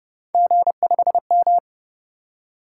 Morse code, heard as a keyed tone, about 32 wpm, 700 Hz. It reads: G5M